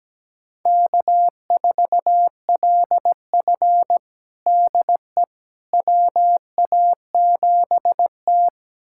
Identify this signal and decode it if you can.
Morse code, heard as a keyed tone, about 17 wpm, 700 Hz. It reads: K4LF DE WA7T
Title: USA Callsign DE Callsign - 4 Characters - 17wpm